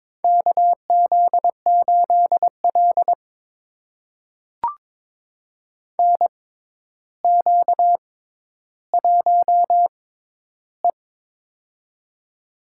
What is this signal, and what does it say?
Morse code, heard as a keyed tone, about 22 wpm, spaced as for 11 wpm, 700 Hz. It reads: KZ8L NQ1E
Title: USA Call Signs - 6x Character Spacing - 22wpm